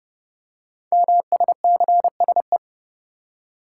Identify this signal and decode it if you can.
Morse code, heard as a keyed tone, about 30 wpm, 700 Hz. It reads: MSCSE
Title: Code Groups - 5 Letter:Numbers - 30wpm